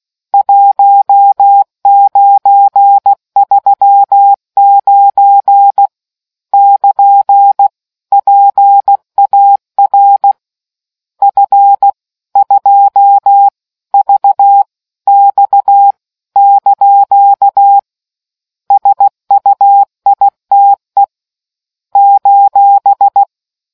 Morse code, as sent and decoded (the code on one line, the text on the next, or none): .---- ----. ...-- ----. / -.--. / .--. .- .-. / ..-. / ..--- / ...- / -..- / -.--.- / ... ..- .. - . / ---...
1939 <KN> PAR F 2 V X ) SUITE :